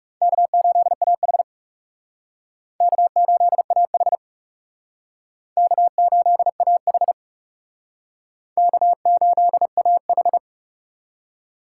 Morse code, heard as a keyed tone, about 37 wpm, 700 Hz. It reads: K8AH K8AH K8AH K8AH
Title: Speed Racing - USA 1x2 Call Signs 30wpm